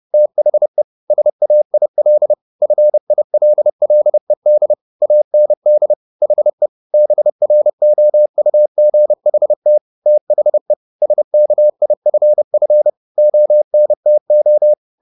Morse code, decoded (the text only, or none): THE SAIL FILLED AND HE BROUGHT THE SKIFF ONTO